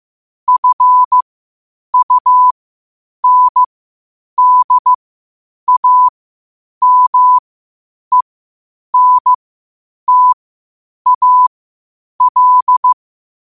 Morse code, heard as a keyed tone, about 15 wpm, spaced as for 8 wpm, 1000 Hz. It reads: FUNDAMENTAL
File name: qotdfiles / 8wpm